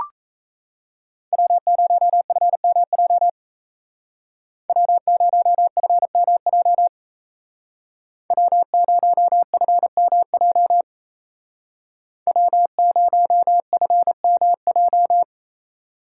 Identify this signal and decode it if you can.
Morse code, heard as a keyed tone, about 33 wpm, 700 Hz. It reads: W0FMJ W0FMJ W0FMJ W0FMJ
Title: Speed Racing - USA 1x3 Call Signs 28wpm